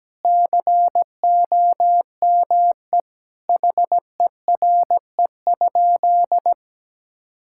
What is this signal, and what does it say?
Morse code, heard as a keyed tone, about 17 wpm, 700 Hz. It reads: COME HERE?
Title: Sentences from Top 1000 Words - Review Part 5 - 17wpm